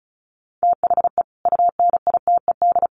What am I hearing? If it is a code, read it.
Morse code, heard as a keyed tone, about 34 wpm, 700 Hz. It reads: THE UNITED
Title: Sets of 3 Words 35wpm